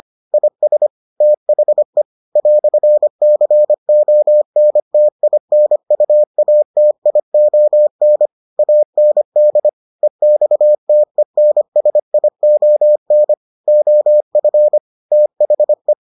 Morse code, heard as a keyed tone, about 25 wpm, 600 Hz. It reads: IS THE "CONTINUATION AND EXTENSION OF THE